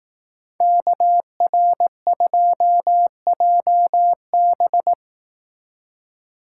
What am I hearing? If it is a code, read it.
Morse code, heard as a keyed tone, about 18 wpm, 700 Hz. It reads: KR2JB
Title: Speed Racing - USA 2x2 Call Signs 15wpm